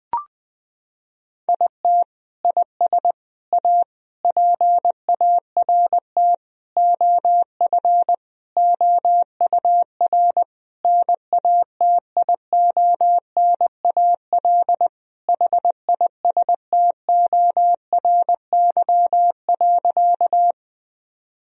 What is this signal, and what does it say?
Morse code, heard as a keyed tone, about 20 wpm, 700 Hz. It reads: IT IS A PART OF OUR NATIONAL HISTORY.
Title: Sentences from Top 1000 Words - Review Part 4 - 20wpm